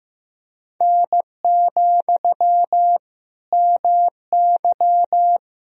Morse code, heard as a keyed tone, about 16 wpm, 700 Hz. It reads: N, MY
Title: Sentences from Top 400 Words and Abbreviations 15wpm